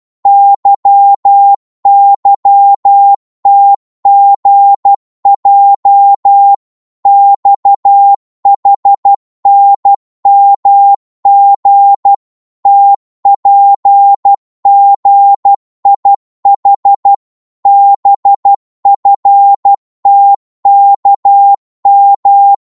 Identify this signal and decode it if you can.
Morse code, heard as a keyed tone, about 12 wpm, 800 Hz. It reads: YYTGJ XHNMG TPGIH BFTKM
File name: Morse / ltr-12wpm